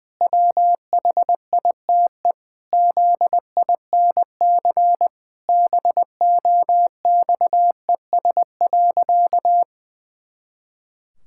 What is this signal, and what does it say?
Morse code, heard as a keyed tone, about 20 wpm, 700 Hz. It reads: WHITE ZINC BOXES.
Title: Pangram Challenge v2 - 20wpm